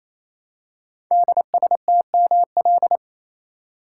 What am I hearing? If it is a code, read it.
Morse code, heard as a keyed tone, about 27 wpm, 700 Hz. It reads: DSTML
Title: Code Groups - 5 Letters - 28wpm